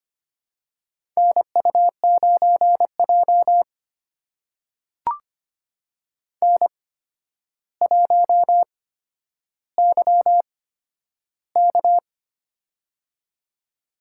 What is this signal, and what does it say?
Morse code, heard as a keyed tone, about 25 wpm, spaced as for 11 wpm, 700 Hz. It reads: NU9J N1YK
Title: USA Call Signs - 4 Characters - 8x Character Spacing - 25wpm